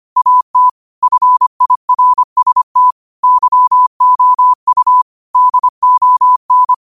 Morse code, heard as a keyed tone, about 25 wpm, 1000 Hz. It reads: AT FIRST YOU DON
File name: qotdfiles / 25wpm